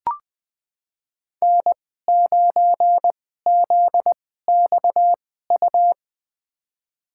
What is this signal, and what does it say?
Morse code, heard as a keyed tone, about 21 wpm, spaced as for 15 wpm, 700 Hz. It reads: N9ZXU
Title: USA Call Signs - 5 Characters - 2x Character Spacing - 20wpm